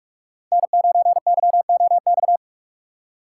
Morse code, heard as a keyed tone, about 46 wpm, 700 Hz. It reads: N9YOX
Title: USA Call Signs - 5 Characters - 3x Character Spacing - 45wpm